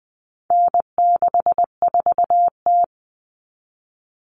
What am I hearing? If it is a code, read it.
Morse code, heard as a keyed tone, about 20 wpm, 700 Hz. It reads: N64T
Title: Common Cut Numbers - 4-Digit 20wpm